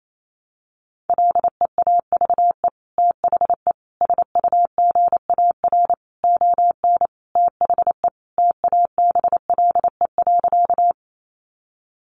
Morse code, heard as a keyed tone, about 28 wpm, 700 Hz. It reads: LEAVE THE SUGAR ON THE TABLE.
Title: Sentences from Top 1200 Words 28wpm